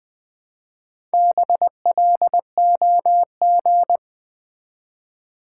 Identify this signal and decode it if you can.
Morse code, heard as a keyed tone, about 20 wpm, 700 Hz. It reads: BLOG
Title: Build A Word - 1 Letter At A Time - 4-Letter Words - 20wpm